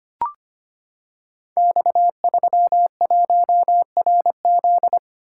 Morse code, heard as a keyed tone, about 25 wpm, 700 Hz. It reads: X31RZ
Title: Code Groups - 5 Letter:Numbers - 25wpm